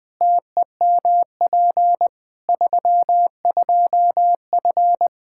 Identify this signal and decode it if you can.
Morse code, heard as a keyed tone, about 21 wpm, 700 Hz. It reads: TEMP 32F
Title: K1USN SST QSO Practice 20wpm